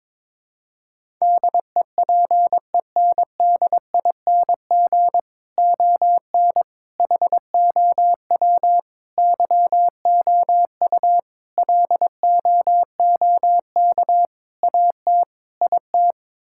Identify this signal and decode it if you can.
Morse code, heard as a keyed tone, about 22 wpm, 700 Hz. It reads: DEPENDING ON HOW YOU LOOK AT IT
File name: practice / 22wpm / Sets of 7 Words - 22wpm